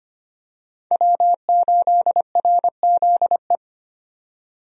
Morse code, heard as a keyed tone, about 25 wpm, 700 Hz. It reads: W8RZE
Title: USA Call Signs - 5 Characters - 25wpm